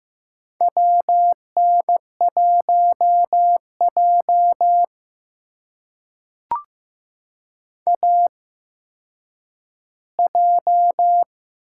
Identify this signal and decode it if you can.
Morse code, heard as a keyed tone, about 15 wpm, spaced as for 9 wpm, 700 Hz. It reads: WN1J AJ
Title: USA Call Signs - 4 Characters - 8x Character Spacing - 15wpm